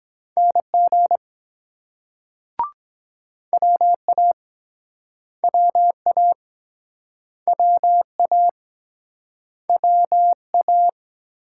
Morse code, heard as a keyed tone, about 22 wpm, 700 Hz. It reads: NG WA WA WA WA